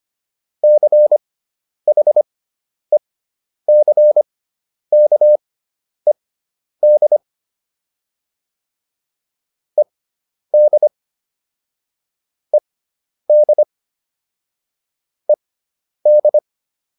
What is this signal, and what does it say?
Morse code, heard as a keyed tone, about 25 wpm, spaced as for 10 wpm, 600 Hz. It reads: CHECKED ED ED ED